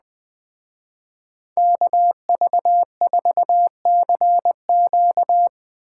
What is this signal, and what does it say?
Morse code, heard as a keyed tone, about 20 wpm, 700 Hz. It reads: KV4CQ